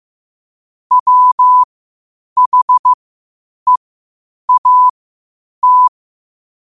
Morse code, heard as a keyed tone, about 15 wpm, spaced as for 8 wpm, 1000 Hz. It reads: WHEAT
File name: qotdfiles / 8wpm